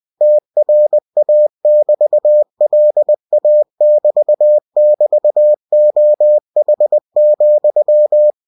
Morse code, heard as a keyed tone, about 20 wpm, 600 Hz. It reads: TRA<BT>LA<BT><BT>OH,